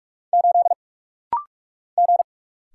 Morse code, heard as a keyed tone, about 44 wpm, 700 Hz. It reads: 8 G
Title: Single Letter-Number - Rapid-Fire 45wpm